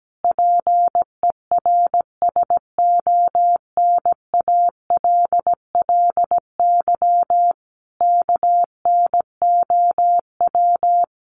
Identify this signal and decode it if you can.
Morse code, heard as a keyed tone, about 17 wpm, 700 Hz. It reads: PERSONALLY KNOW
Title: Sentences from Top 1400 Words 17wpm